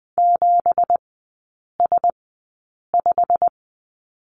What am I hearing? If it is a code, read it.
Morse code, heard as a keyed tone, about 20 wpm, spaced as for 8 wpm, 700 Hz. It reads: 7S5
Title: 178 Lesson 29e - Slash - Three Characters 20wpm